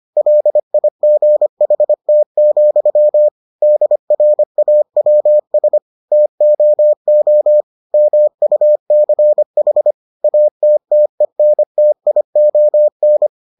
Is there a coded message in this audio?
Yes: Morse code from 0.2 to 13.3 s, heard as a keyed tone, about 25 wpm, 600 Hz. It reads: LIGHT, DRAWS TOO MUCH ATTENTION